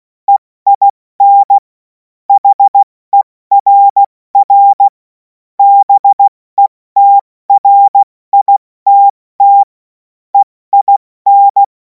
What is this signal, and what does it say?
Morse code, heard as a keyed tone, about 16 wpm, 800 Hz. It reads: EIN HERR BETRITT EIN
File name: Morsekurs / Witze / Witz 16. Juni 2026 70 F80